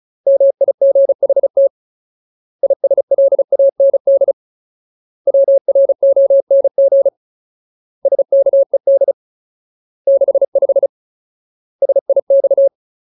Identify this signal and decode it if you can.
Morse code, heard as a keyed tone, about 34 wpm, 550 Hz. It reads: MIGHT ISLAND WRONG SKED 65 SIX
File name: Ultimate Word List / 35WPM Ultimate